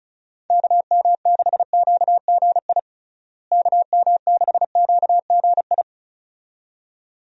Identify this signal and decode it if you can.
Morse code, heard as a keyed tone, about 36 wpm, 700 Hz. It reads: KM6QGI KM6QGI
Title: USA Call Signs - 6 Characters - Repeated 2x - 35wpm